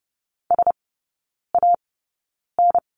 Morse code, heard as a keyed tone, about 30 wpm, spaced as for 9 wpm, 700 Hz. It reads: SAN
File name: Lesson 17e - Letter u - Three Characters 30wpm